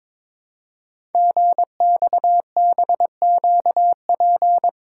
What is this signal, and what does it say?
Morse code, heard as a keyed tone, about 22 wpm, 700 Hz. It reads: GXBQP
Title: Code Groups - 5 Letters - 22wpm